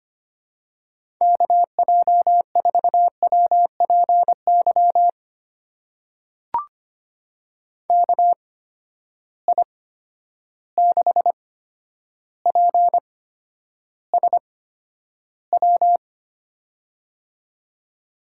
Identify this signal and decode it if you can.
Morse code, heard as a keyed tone, about 25 wpm, spaced as for 11 wpm, 700 Hz. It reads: KJ4WPY KI6PSW
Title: USA Call Signs - 6 Characters - 8x Character Spacing - 25wpm